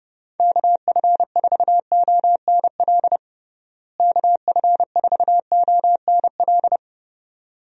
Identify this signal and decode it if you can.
Morse code, heard as a keyed tone, about 30 wpm, 700 Hz. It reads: KF4ONL KF4ONL
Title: USA Call Signs - 6 Characters - Repeated 2x - 30wpm